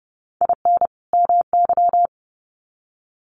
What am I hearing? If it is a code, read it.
Morse code, heard as a keyed tone, about 30 wpm, 700 Hz. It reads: IN MY
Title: Sets of 2 Words 30wpm